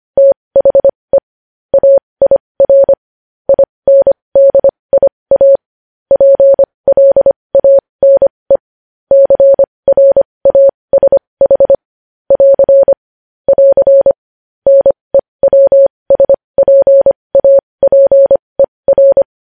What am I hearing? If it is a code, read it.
Morse code, heard as a keyed tone, about 25 wpm, spaced as for 20 wpm, 550 Hz. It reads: THE AIR INDIA PLANE CRASH <AR> <AR> NEWSPAPER
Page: News headlines in Morse code at 20 wpm